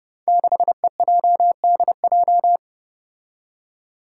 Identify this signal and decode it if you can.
Morse code, heard as a keyed tone, about 29 wpm, 700 Hz. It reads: 6EJDJ
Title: Five Challenging Letter-Number Combinations 30wpm